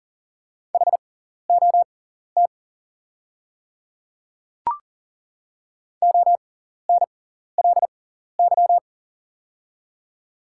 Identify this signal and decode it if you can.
Morse code, heard as a keyed tone, about 39 wpm, spaced as for 14 wpm, 700 Hz. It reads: HOT ONLY
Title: 6x Character Spacing - Top 200 Words - 40wpm